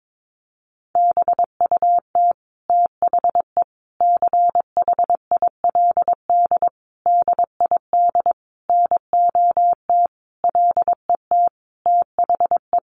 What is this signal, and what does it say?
Morse code, heard as a keyed tone, about 22 wpm, 700 Hz. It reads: BUT THE CHILD DID NOT LET THE
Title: Sentences from Top 1400 Words 22wpm